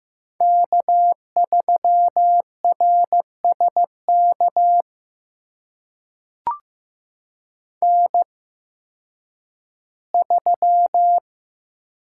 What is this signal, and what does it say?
Morse code, heard as a keyed tone, about 15 wpm, spaced as for 10 wpm, 700 Hz. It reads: K3RSK N3
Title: USA Call Signs - 5 Characters - 8x Character Spacing - 15wpm